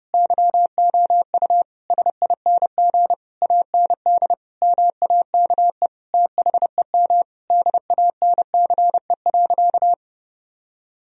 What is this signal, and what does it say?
Morse code, heard as a keyed tone, about 30 wpm, 700 Hz. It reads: YOU SING AND MAKE THEM DANCE.
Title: Sentences from Top 1000 Words - Review Part 4 - 30wpm